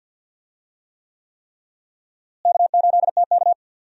Morse code, heard as a keyed tone, about 52 wpm, 700 Hz. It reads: K8TK
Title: USA Call Signs - 4 Characters - Repeated 3x - 50wpm